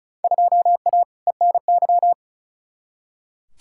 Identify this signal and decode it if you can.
Morse code, heard as a keyed tone, about 36 wpm, 700 Hz. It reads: 2A ENY
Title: Field Day Exchanges Encore - 35wpm